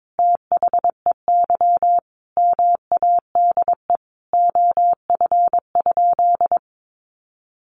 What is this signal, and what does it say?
Morse code, heard as a keyed tone, about 22 wpm, 700 Hz. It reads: THEY MADE OF?
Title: Sentences from Top 200 Words 22wpm